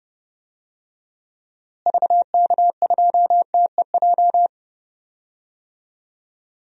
Morse code, heard as a keyed tone, about 31 wpm, 700 Hz. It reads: VK2TEJ
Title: DXCC-100 Call Signs - 6 Characters - Repeated 2x - 30wpm